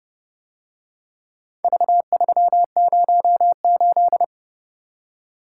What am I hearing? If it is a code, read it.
Morse code, heard as a keyed tone, about 30 wpm, 700 Hz. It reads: V308